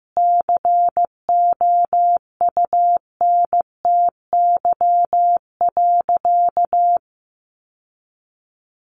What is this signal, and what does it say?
Morse code, heard as a keyed tone, about 14 wpm, 700 Hz. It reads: COUNTY.
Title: Sentences from Top 1200 Words 15wpm